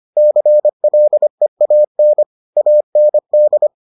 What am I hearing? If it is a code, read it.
Morse code, heard as a keyed tone, about 25 wpm, 600 Hz. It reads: CLEAN AND